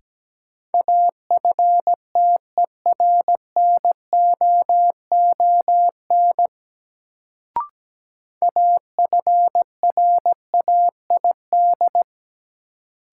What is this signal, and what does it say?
Morse code, heard as a keyed tone, about 17 wpm, 700 Hz. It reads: AFTERNOON AFRAID
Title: Sentences from Top 1500 Words - New Word Drill - 17wpm